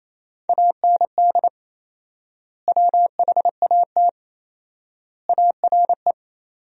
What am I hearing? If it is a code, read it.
Morse code, heard as a keyed tone, about 28 wpm, 700 Hz. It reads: AND WHAT ARE
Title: 4x Word-Spacing - Sentences from Top 200 Words - 28wpm